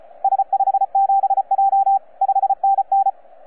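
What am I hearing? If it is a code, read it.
Morse code, heard as a keyed tone, about 36 wpm, 780 Hz, with radio noise. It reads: S57J 5NN
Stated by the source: FS/LJ2IJ heard on 12 metres CW, from DX World.net.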